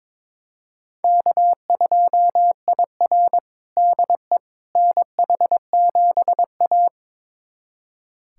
Morse code, heard as a keyed tone, about 22 wpm, 700 Hz. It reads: K2IR DE NH7A
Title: USA Callsign DE Callsign - 4 Characters - 22wpm